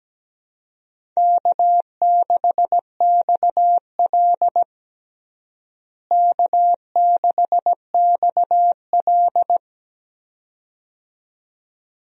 Morse code, heard as a keyed tone, about 17 wpm, 700 Hz. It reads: K6XL K6XL
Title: USA Call Signs - 4 Characters - Repeated 2x - 17wpm